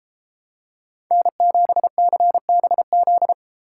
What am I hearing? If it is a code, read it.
Morse code, heard as a keyed tone, about 28 wpm, 700 Hz. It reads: N7CBZ